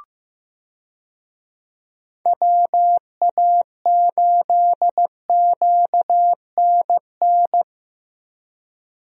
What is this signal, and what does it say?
Morse code, heard as a keyed tone, about 15 wpm, 700 Hz. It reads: WA8QNN
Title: USA Call Signs - 6 Characters - Repeated 3x - 15wpm